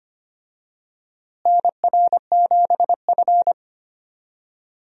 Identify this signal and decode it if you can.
Morse code, heard as a keyed tone, about 25 wpm, 700 Hz. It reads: NR7F
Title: USA Call Signs - 4 Characters - Repeated 2x - 25wpm